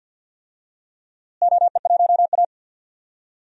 Morse code, heard as a keyed tone, about 46 wpm, 700 Hz. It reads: OE1A
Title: DXCC-100 Call Signs - 4 Characters - Repeated 2x - 50wpm